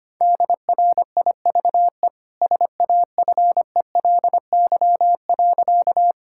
Morse code, heard as a keyed tone, about 25 wpm, 700 Hz. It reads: DRIVE SAFELY.
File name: Easy Sentences with Hints - 25wpm